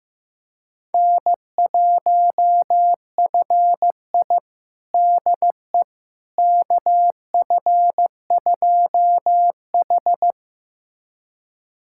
Morse code, heard as a keyed tone, about 15 wpm, 700 Hz. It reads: N1FI DE KF2H
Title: USA Callsign DE Callsign - 4 Characters - 15wpm